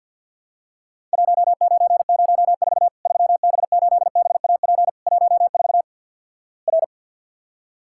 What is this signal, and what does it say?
Morse code, heard as a keyed tone, about 50 wpm, 700 Hz. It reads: 1904 3B8BAP 14 R
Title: Worked All Europe DX Contest - QTC Exchange - 50wpm